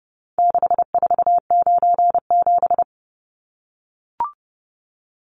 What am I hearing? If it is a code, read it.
Morse code, heard as a keyed tone, about 30 wpm, 700 Hz. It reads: 6497
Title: Numbers - 4-Digit 30wpm